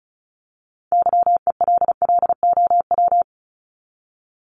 Morse code, heard as a keyed tone, about 34 wpm, 700 Hz. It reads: YELLOW
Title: Top 1751-2000 Words 35wpm